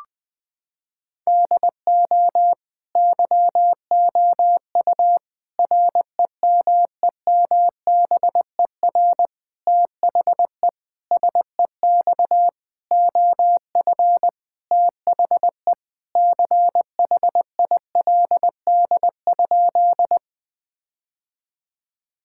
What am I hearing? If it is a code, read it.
Morse code, heard as a keyed tone, about 20 wpm, 700 Hz. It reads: DO YOU REMEMBER THE SEX OF THE CHILD?
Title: Sentences from Top 1600 Words 20wpm